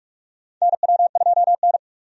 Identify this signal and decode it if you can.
Morse code, heard as a keyed tone, about 46 wpm, 700 Hz. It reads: NW2N
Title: USA Call Signs - 4 Characters - 45wpm